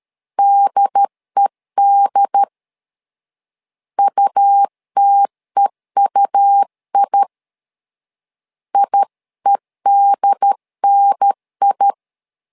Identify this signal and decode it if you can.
Morse code, heard as a keyed tone, about 13 wpm, 800 Hz. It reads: DED UTEUI IEDNI
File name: LEZIONI / 2 UHDN / 2 UHDN 12wpm 1456